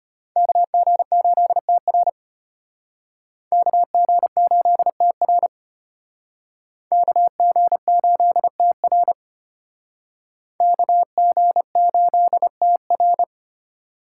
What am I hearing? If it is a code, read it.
Morse code, heard as a keyed tone, about 33 wpm, 700 Hz. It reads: KG8TR KG8TR KG8TR KG8TR